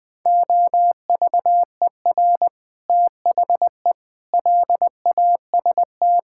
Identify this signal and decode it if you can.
Morse code, heard as a keyed tone, about 20 wpm, 700 Hz. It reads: OVER THE LAST
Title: Sets of 6 Words - 20wpm